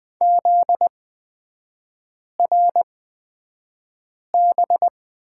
Morse code, heard as a keyed tone, about 20 wpm, spaced as for 5 wpm, 700 Hz. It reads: ZRB
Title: 243 Lesson 39e - Letter z - Three Characters 20wpm